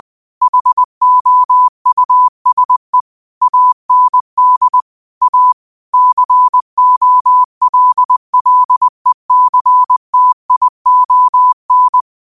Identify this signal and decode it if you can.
Morse code, heard as a keyed tone, about 20 wpm, 1000 Hz. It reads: HOUSE AND A COLLECTION
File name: qotdfiles / 20wpm